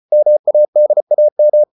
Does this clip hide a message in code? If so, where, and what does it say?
Yes: Morse code from 0.1 to 1.6 s, heard as a keyed tone, about 34 wpm, 600 Hz. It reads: MADAM